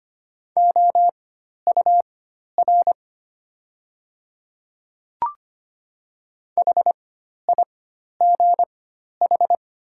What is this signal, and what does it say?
Morse code, heard as a keyed tone, about 25 wpm, spaced as for 12 wpm, 700 Hz. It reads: OUR HIGH
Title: 4x Character Spacing - Top 300 Words - 25wpm